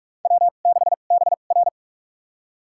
Morse code, heard as a keyed tone, about 46 wpm, spaced as for 32 wpm, 700 Hz. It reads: W6BR
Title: USA Call Signs - 2x Character Spacing - 45wpm